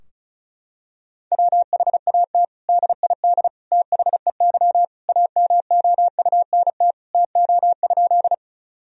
Morse code, heard as a keyed tone, about 36 wpm, 700 Hz. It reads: WHAT DID THEY AMOUNT TO?
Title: Sentences from Top 1000 Words - Review Part 4 - 35wpm